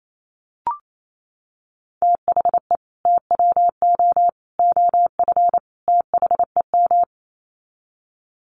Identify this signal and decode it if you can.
Morse code, heard as a keyed tone, about 28 wpm, 700 Hz. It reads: THE TWO OF THEM
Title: Sets of 4 Words 28wpm